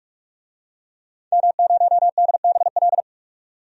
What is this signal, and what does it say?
Morse code, heard as a keyed tone, about 41 wpm, 700 Hz. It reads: M0DBL
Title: DXCC-100 Call Signs - 5 Characters - 45wpm